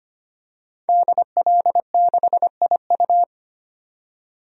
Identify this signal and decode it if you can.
Morse code, heard as a keyed tone, about 25 wpm, 700 Hz. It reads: DL6IU
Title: DXCC-100 Call Signs - 5 Characters - 25wpm